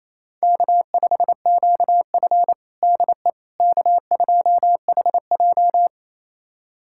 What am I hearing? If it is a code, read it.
Morse code, heard as a keyed tone, about 28 wpm, 700 Hz. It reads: K5QF DE K2HJ
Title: USA Callsign DE Callsign - 4 Characters - 28wpm